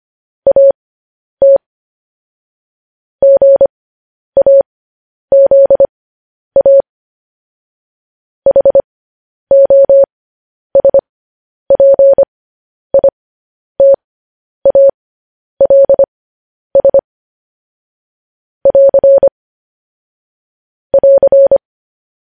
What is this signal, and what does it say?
Morse code, heard as a keyed tone, about 25 wpm, spaced as for 10 wpm, 550 Hz. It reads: AT GAZA HOSPITALS <AR> <AR>
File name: podcast / News headlines-1x-25x10